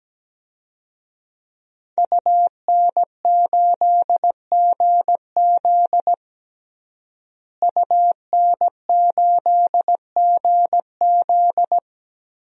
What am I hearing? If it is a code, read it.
Morse code, heard as a keyed tone, about 17 wpm, 700 Hz. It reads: UN8GZ UN8GZ
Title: DXCC-100 Call Signs - 5 Characters - Repeated 3x - 17wpm